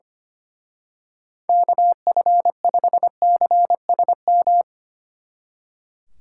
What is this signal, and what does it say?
Morse code, heard as a keyed tone, about 27 wpm, 700 Hz. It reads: KF5CSM
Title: Speed Racing - USA 2x3 Call Signs 25wpm